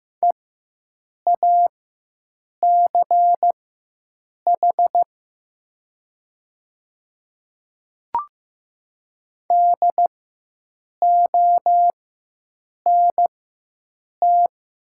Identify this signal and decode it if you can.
Morse code, heard as a keyed tone, about 15 wpm, spaced as for 7 wpm, 700 Hz. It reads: EACH DONT
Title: 4x Character Spacing - Top 500 Words - 15wpm